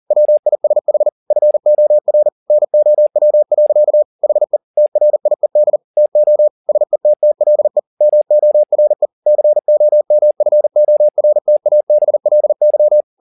Morse code, heard as a keyed tone, about 40 wpm, 600 Hz. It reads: WISH FOR NOW. HE TRIED TO SETTLE MORE COMFORTABLY
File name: the old man and the sea 2 40wpm